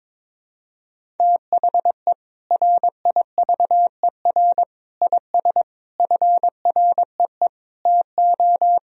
Morse code, heard as a keyed tone, about 22 wpm, 700 Hz. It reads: THE RIVER IS FREE TO
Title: Sentences from Top 1000 Words - Review Part 1 - 22wpm